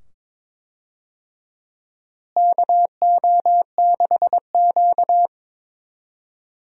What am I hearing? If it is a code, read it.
Morse code, heard as a keyed tone, about 22 wpm, 700 Hz. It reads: KO6Q
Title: USA Call Signs - 4 Characters - Repeated 2x - 22wpm